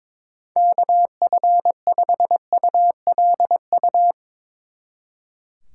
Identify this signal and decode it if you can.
Morse code, heard as a keyed tone, about 22 wpm, 700 Hz. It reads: KF5ULU
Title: Speed Racing - USA 2x3 Call Signs 22wpm